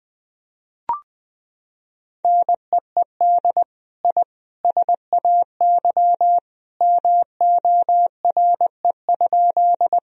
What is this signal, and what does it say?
Morse code, heard as a keyed tone, about 20 wpm, 700 Hz. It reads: NEED I SAY MORE?
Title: Sentences from Top 1000 Words - Review Part 1 - 20wpm